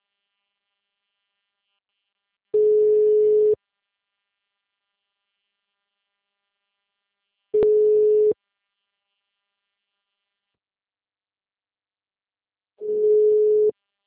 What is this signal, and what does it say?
Ringback tone (Europe, 425 Hz).